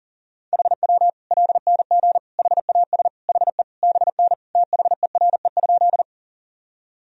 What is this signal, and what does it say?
Morse code, heard as a keyed tone, about 40 wpm, 700 Hz. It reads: HW LNG HAS HE BN THERE?
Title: Sentences from Top 100 Words and Abbreviations 40wpm